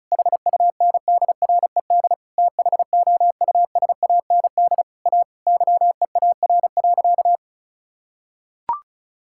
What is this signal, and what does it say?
Morse code, heard as a keyed tone, about 34 wpm, 700 Hz. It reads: HUNDRED THOUSAND A YEAR.
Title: Sentences from Top 1000 Words - Review Part 5 - 35wpm